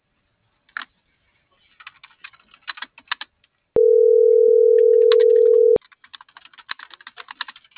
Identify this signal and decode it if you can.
Ringback tone (North America), 440 + 480 Hz.